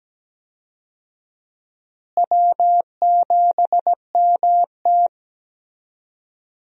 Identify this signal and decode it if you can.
Morse code, heard as a keyed tone, about 17 wpm, 700 Hz. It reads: W7MT
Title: USA Call Signs - 4 Characters - Repeated 2x - 17wpm